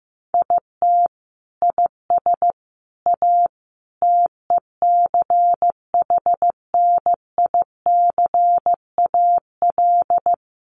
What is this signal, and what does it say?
Morse code, heard as a keyed tone, about 15 wpm, 700 Hz. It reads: IT IS A TECHNICAL
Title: Sentences from Top 1100 Words 15wpm